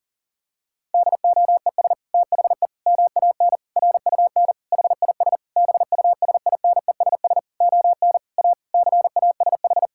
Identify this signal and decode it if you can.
Morse code, heard as a keyed tone, about 37 wpm, 700 Hz. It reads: DOES THE MAN RUN HIS BUSINESS ON A CASH
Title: Sentences from Top 1000 Words - Review Part 5 - 40wpm